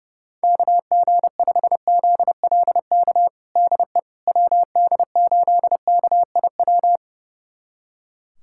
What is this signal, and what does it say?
Morse code, heard as a keyed tone, about 30 wpm, 700 Hz. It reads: KG5ZLK DE WD8KIW
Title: USA Callsign DE Callsign - 6 Characters - 30wpm